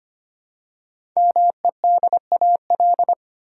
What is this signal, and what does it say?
Morse code, heard as a keyed tone, about 25 wpm, 700 Hz. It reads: MEDAL
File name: 5 Letter Words - First 3 Letters Given - 25wpm